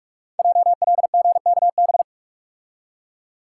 Morse code, heard as a keyed tone, about 44 wpm, 700 Hz. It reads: JLGKB
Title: Code Groups - 5 Letters - 45wpm